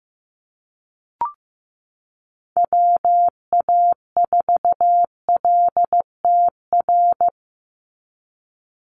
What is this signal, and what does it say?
Morse code, heard as a keyed tone, about 15 wpm, 700 Hz. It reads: WA4LTR
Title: USA Call Signs 15wpm